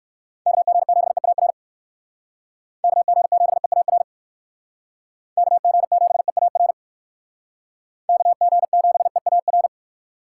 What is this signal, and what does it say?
Morse code, heard as a keyed tone, about 68 wpm, 700 Hz. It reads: KG7EAR KG7EAR KG7EAR KG7EAR